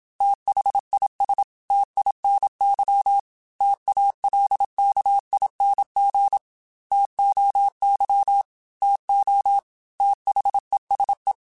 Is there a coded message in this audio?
Yes: Morse code from 0.2 to 11.3 s, heard as a keyed tone, about 27 wpm, 800 Hz. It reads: THIS TINY TALKING TOY TO THESE